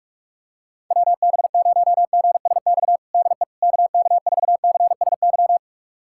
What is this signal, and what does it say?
Morse code, heard as a keyed tone, about 41 wpm, 700 Hz. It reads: WB0GSX DE KK4CSY
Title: USA Callsign DE Callsign - 6 Characters - 45wpm